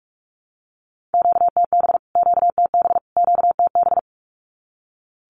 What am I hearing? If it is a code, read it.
Morse code, heard as a keyed tone, about 44 wpm, 700 Hz. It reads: QTB QTB QTB
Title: Q-Codes Repeated 3x 45wpm